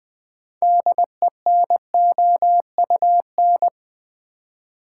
Morse code, heard as a keyed tone, about 20 wpm, 700 Hz. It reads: DENOUN
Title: Build A Word - 3 Letters At A Time - 8-Letter Words - 20wpm